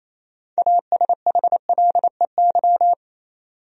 Morse code, heard as a keyed tone, about 28 wpm, 700 Hz. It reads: ASHLEY
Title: USA Common Names - 28wpm